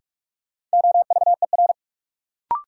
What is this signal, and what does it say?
Morse code, heard as a keyed tone, about 46 wpm, 700 Hz. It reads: OVER
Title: Sentences from Top 100 Words - New Word Drill - 45wpm